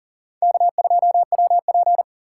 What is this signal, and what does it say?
Morse code, heard as a keyed tone, about 40 wpm, 700 Hz. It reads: K2WP
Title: USA Call Signs - 6x Character Spacing - 40wpm